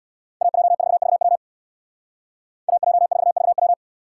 Morse code, heard as a keyed tone, about 64 wpm, 700 Hz. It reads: AJ4VW AJ4VW